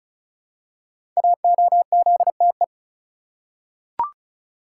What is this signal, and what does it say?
Morse code, heard as a keyed tone, about 36 wpm, 700 Hz. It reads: AOZTE
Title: Code Groups - 5 Letters - 35wpm